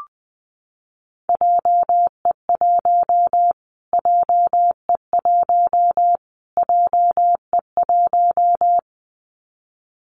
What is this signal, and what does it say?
Morse code, heard as a keyed tone, about 20 wpm, 700 Hz. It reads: JE1 JE1 JE1
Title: Callsign Prefixes for Top 25 DXCC - Repeated 3x - 20wpm